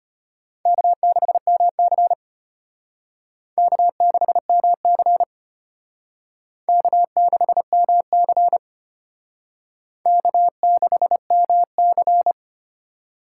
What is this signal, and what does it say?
Morse code, heard as a keyed tone, about 32 wpm, 700 Hz. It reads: K6MC K6MC K6MC K6MC